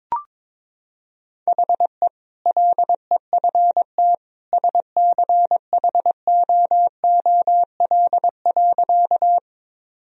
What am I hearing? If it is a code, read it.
Morse code, heard as a keyed tone, about 22 wpm, 700 Hz. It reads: HE LEFT SCHOOL.
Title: Sentences from Top 1000 Words - Review Part 1 - 22wpm